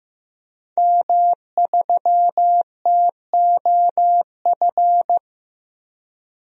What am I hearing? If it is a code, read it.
Morse code, heard as a keyed tone, about 15 wpm, 700 Hz. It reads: M3TOF
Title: DXCC-100 Call Signs - 5 Characters - 15wpm